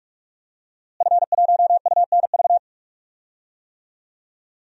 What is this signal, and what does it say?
Morse code, heard as a keyed tone, about 46 wpm, 700 Hz. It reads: F1UNV
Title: DXCC-100 Call Signs - 5 Characters - Repeated 3x - 45wpm